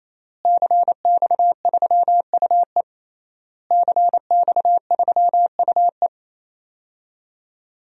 Morse code, heard as a keyed tone, about 27 wpm, 700 Hz. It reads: CX3UE CX3UE
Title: DXCC-100 Call Signs - 5 Characters - Repeated 3x - 28wpm